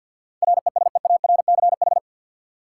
Morse code, heard as a keyed tone, about 50 wpm, 700 Hz. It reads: RESEARCH
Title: Sentences from Top 1000 Words - Review Part 4 - New Word Drill - 50wpm